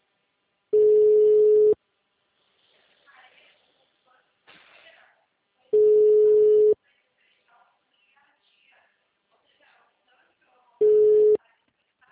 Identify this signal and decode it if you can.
Ringback tone (Europe, 425 Hz), with speech.